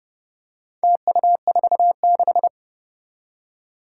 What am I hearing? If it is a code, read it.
Morse code, heard as a keyed tone, about 30 wpm, 700 Hz. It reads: TU46
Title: Four Challenging Letter-Number Combinations 30wpm